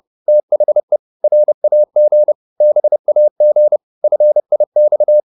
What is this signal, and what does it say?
Morse code, heard as a keyed tone, about 30 wpm, 600 Hz. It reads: THE RAG BAG FIX
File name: PR204 30